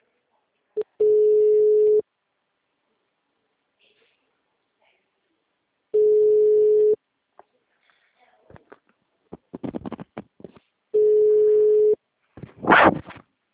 Ringback tone (Europe, 425 Hz).
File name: out- -2012- - - .1480